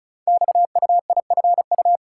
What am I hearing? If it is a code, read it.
Morse code, heard as a keyed tone, about 36 wpm, 700 Hz. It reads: XUIFU
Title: Code Groups - 5 Letters - 35wpm